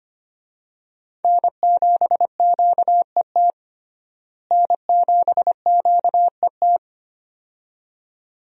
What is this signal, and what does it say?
Morse code, heard as a keyed tone, about 25 wpm, 700 Hz. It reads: N7QET N7QET
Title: USA Call Signs - 5 Characters - Repeated 2x - 25wpm